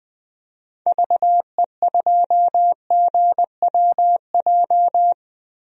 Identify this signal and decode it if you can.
Morse code, heard as a keyed tone, about 20 wpm, 700 Hz. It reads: VE2GWJ
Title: DXCC-100 Call Signs - 6 Characters - 20wpm